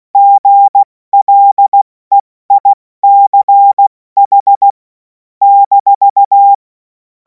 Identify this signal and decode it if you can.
Morse code, heard as a keyed tone, about 16 wpm, 800 Hz. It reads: GLEICH -
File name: Morsekurs / Witze / Witz 28. April 2026 70 F80